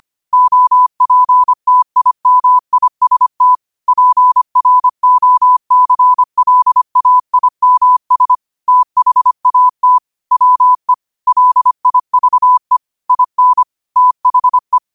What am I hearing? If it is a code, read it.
Morse code, heard as a keyed tone, about 25 wpm, 1000 Hz. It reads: OPTIMIST PROCLAIMS THAT WE LIVE IN THE
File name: qotdfiles / 25wpm